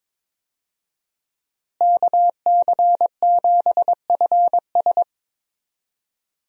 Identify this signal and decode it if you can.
Morse code, heard as a keyed tone, about 22 wpm, 700 Hz. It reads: KC7FS
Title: USA Call Signs - 5 Characters - Repeated 3x - 22wpm